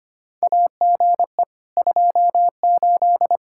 Morse code, heard as a keyed tone, about 25 wpm, 700 Hz. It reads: AGE 28
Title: K1USN SST QSO Practice 25wpm